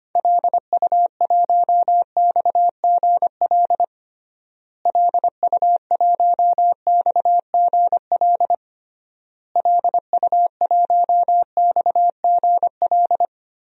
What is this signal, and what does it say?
Morse code, heard as a keyed tone, about 25 wpm, 700 Hz. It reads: LU1XGL LU1XGL LU1XGL
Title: DXCC-100 Call Signs - 6 Characters - Repeated 3x - 25wpm